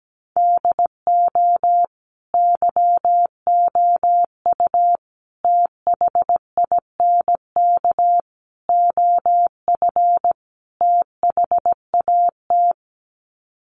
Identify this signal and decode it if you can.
Morse code, heard as a keyed tone, about 17 wpm, 700 Hz. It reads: DO YOU THINK OF THAT
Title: Sets of 5 Words Encore2 17wpm